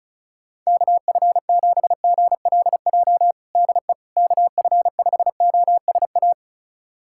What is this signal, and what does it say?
Morse code, heard as a keyed tone, about 34 wpm, 700 Hz. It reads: KF7GLJ DE KF5OSA
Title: USA Callsign DE Callsign - 6 Characters - 35wpm